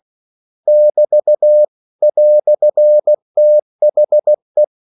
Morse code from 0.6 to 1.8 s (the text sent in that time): <BT>